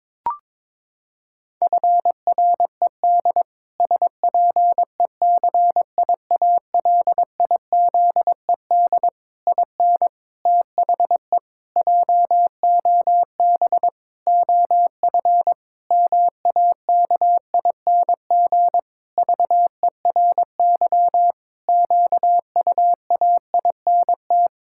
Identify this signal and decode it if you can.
Morse code, heard as a keyed tone, about 22 wpm, 700 Hz. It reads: FRED SPECIALIZED IN THE JOB OF MAKING VERY QUAINT
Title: Pangram Challenge v2 - 22wpm